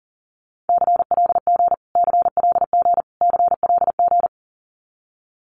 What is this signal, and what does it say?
Morse code, heard as a keyed tone, about 40 wpm, 700 Hz. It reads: CLG CLG CLG
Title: Abbreviations Repeated 3x - 40wpm